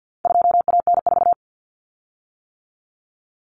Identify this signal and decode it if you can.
Morse code, heard as a keyed tone, about 50 wpm, 700 Hz. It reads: 2AN4
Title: Common Cut Numbers - 4-Digit 50wpm